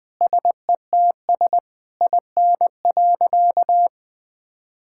Morse code, heard as a keyed tone, about 20 wpm, 700 Hz. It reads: SETS IN.
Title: Difficult Sentences with Hints - 20wpm